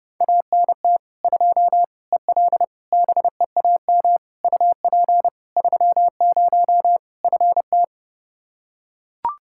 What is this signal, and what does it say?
Morse code, heard as a keyed tone, about 30 wpm, 700 Hz. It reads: ANT 2 EL BEAM UP 30 FT
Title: Practice QSO - Exchange 2 - Info - 30wpm